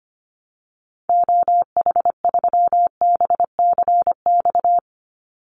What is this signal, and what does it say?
Morse code, heard as a keyed tone, about 25 wpm, 700 Hz. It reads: OH3BCX
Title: Call Signs Repeated 2x - 25wpm